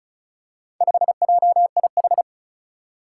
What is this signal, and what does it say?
Morse code, heard as a keyed tone, about 36 wpm, 700 Hz. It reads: 5JIH
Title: Four Challenging Letter-Number Combinations 35wpm